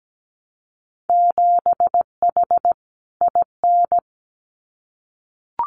Morse code, heard as a keyed tone, about 17 wpm, 700 Hz. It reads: 7H IN
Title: Winter Field Day Exchanges Encore 17wpm